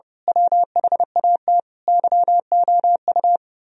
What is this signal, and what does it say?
Morse code, heard as a keyed tone, about 28 wpm, 700 Hz. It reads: WHAT YOU
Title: Speed-Racing - Sets of 5 Words 30wpm